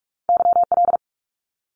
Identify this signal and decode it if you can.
Morse code, heard as a keyed tone, about 44 wpm, 700 Hz. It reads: YL
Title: Abbreviations 45wpm